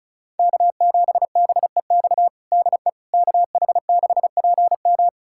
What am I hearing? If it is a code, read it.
Morse code, heard as a keyed tone, about 36 wpm, 700 Hz. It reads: K7BEX DE KH6PM
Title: USA Callsign DE Callsign - 5 Characters - 35wpm